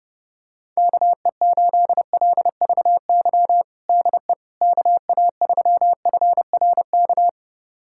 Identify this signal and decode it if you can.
Morse code, heard as a keyed tone, about 30 wpm, 700 Hz. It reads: KE8LVY DE KA3FRK
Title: USA Callsign DE Callsign - 6 Characters - 30wpm